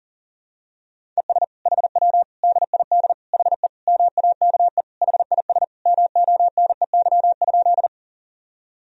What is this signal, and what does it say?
Morse code, heard as a keyed tone, about 40 wpm, 700 Hz. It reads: ES HW DID HE MAKE HIS MONEY?
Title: Sentences from Top 400 Words and Abbreviations 40wpm